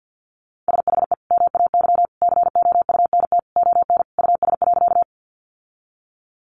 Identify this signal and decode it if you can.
Morse code, heard as a keyed tone, about 50 wpm, 700 Hz. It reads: SHE MAY COUNT ON US.
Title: Sentences from Top 1500 Words 50wpm